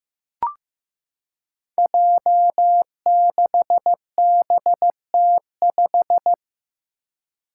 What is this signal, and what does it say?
Morse code, heard as a keyed tone, about 15 wpm, 700 Hz. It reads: J6BT5
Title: Five Challenging Letter-Number Combinations 15wpm